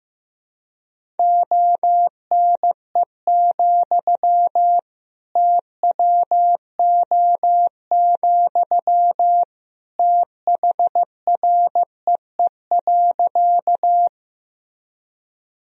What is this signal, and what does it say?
Morse code, heard as a keyed tone, about 15 wpm, 700 Hz. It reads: ONE, TWO, THREE.
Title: Sentences from Top 1000 Words - Review Part 1 - 15wpm